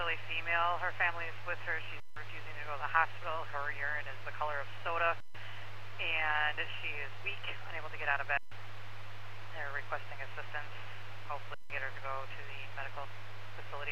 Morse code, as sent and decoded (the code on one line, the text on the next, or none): none